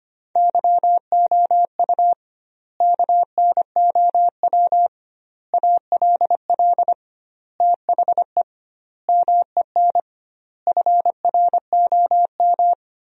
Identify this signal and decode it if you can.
Morse code, heard as a keyed tone, about 25 wpm, 700 Hz. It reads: YOU KNOW ALL THE MEN FROM
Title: 2x Word-Spacing - Sentences from Top 300 Words - 25wpm